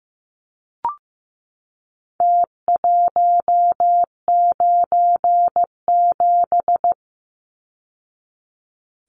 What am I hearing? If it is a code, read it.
Morse code, heard as a keyed tone, about 15 wpm, 700 Hz. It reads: T197
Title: Numbers - Contest Style - 4-Digit 15wpm